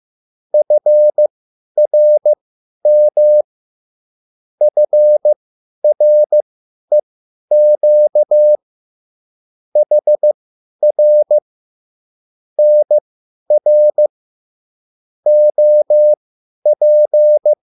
Morse code, heard as a keyed tone, about 15 wpm, spaced as for 10 wpm, 600 Hz. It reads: FRM FREQ HR NR OP
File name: QSL / CW lesson